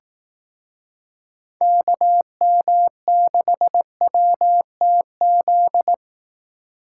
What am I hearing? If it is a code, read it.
Morse code, heard as a keyed tone, about 18 wpm, 700 Hz. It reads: KM6WTZ
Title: Speed Racing - USA Call Signs 15wpm